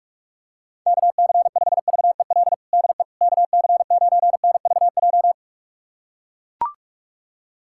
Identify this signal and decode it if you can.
Morse code, heard as a keyed tone, about 44 wpm, 700 Hz. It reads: KC5VEL DE KC9NVJ
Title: USA Callsign DE Callsign - 6 Characters - 45wpm